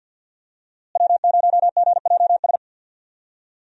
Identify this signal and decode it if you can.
Morse code, heard as a keyed tone, about 47 wpm, 700 Hz. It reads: W0GJS
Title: USA Call Signs - 5 Characters - Repeated 3x - 50wpm